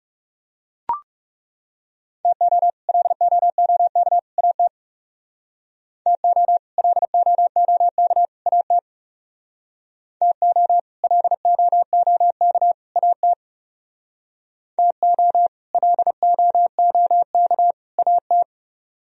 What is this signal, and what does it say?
Morse code, heard as a keyed tone, about 35 wpm, 700 Hz. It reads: TO LOOK AT TO LOOK AT TO LOOK AT TO LOOK AT
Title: Speed-Racing - Sets of 3 Words 30wpm